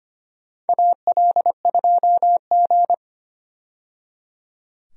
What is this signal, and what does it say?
Morse code, heard as a keyed tone, about 25 wpm, 700 Hz. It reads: AL2G
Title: USA Call Signs - 4 Characters - 25wpm